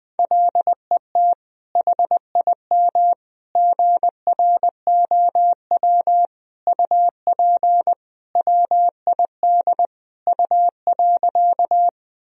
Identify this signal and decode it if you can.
Morse code, heard as a keyed tone, about 20 wpm, 700 Hz. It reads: LET HIM GROW UP WID U.
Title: Sentences from Top 300 Words and Abbreviations 20wpm